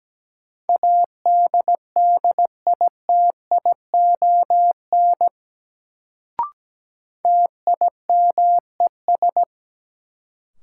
Morse code, heard as a keyed tone, about 17 wpm, 700 Hz. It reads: ADDITION TIMES
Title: Sentences from Top 1000 Words - Review Part 6 - New Word Drill - 17wpm